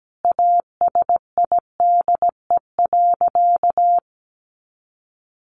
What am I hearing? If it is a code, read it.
Morse code, heard as a keyed tone, about 17 wpm, 700 Hz. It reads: ASIDE.
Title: A Sound Of Thunder 17wpm